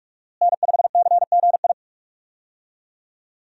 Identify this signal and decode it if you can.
Morse code, heard as a keyed tone, about 46 wpm, 700 Hz. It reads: N5CGI
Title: USA Call Signs - 5 Characters - Repeated 2x - 45wpm